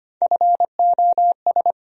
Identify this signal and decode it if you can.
Morse code, heard as a keyed tone, about 25 wpm, 700 Hz. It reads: FOS
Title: USA Call Sign Postfixes - 3 Letters 25wpm